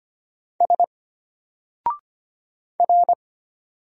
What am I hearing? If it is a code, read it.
Morse code, heard as a keyed tone, about 25 wpm, 700 Hz. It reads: S R
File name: Single-Letter-Number-25wpm